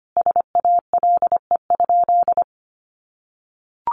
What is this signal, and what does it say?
Morse code, heard as a keyed tone, about 25 wpm, 700 Hz. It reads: SALE?
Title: Sentences from Top 1400 Words 25wpm